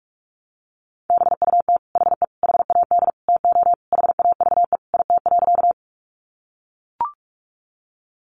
Morse code, heard as a keyed tone, about 44 wpm, 700 Hz. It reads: BUT HE HAD TO HAVE IT.
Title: Sentences from Top 100 Words 45wpm